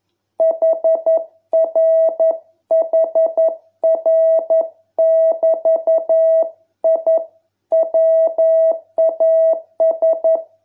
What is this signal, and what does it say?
Morse code, heard as a keyed tone, about 11 wpm, 660 Hz. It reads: HRHR<BT>I WAS